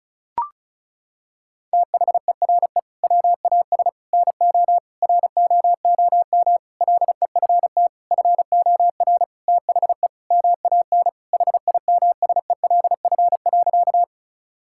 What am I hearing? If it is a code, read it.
Morse code, heard as a keyed tone, about 34 wpm, 700 Hz. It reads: THERE WAS NO ROOM LEFT FOR THE MAN HIMSELF.
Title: Sentences from Top 1600 Words 35wpm